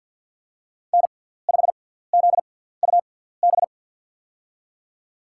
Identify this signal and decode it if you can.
Morse code, heard as a keyed tone, about 48 wpm, spaced as for 17 wpm, 690 Hz. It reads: N5ZUB
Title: USA Call Signs - 6x Character Spacing - 50wpm